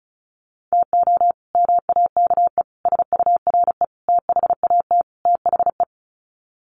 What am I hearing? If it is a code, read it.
Morse code, heard as a keyed tone, about 34 wpm, 700 Hz. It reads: TO MAKE SURE THAT THE
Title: Sets of 5 Words 35wpm